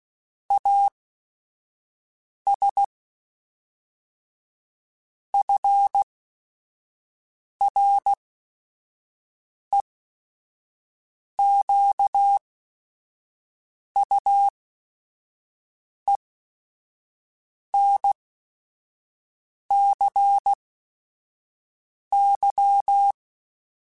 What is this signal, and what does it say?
Morse code, heard as a keyed tone, about 16 wpm, spaced as for 5 wpm, 800 Hz. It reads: AS FREQUENCY
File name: Technical Subjects / Technical Subjects-mp3s / XC @5